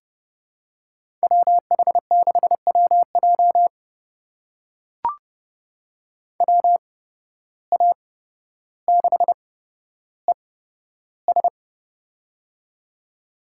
Morse code, heard as a keyed tone, about 32 wpm, spaced as for 13 wpm, 700 Hz. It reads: WH6WJ WA6ES
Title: USA Call Signs - 5 Characters - 8x Character Spacing - 30wpm